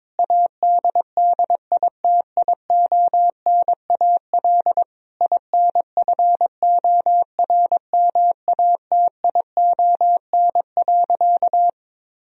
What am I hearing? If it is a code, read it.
Morse code, heard as a keyed tone, about 21 wpm, 700 Hz. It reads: ADDITIONAL INFORMATION.
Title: Sentences from Top 1000 Words - Review Part 4 - 22wpm